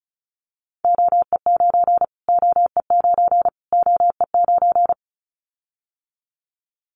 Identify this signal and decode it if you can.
Morse code, heard as a keyed tone, about 34 wpm, 700 Hz. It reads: OE9 OE9 OE9
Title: Callsign Prefixes for Top 25 DXCC - Repeated 3x - 35wpm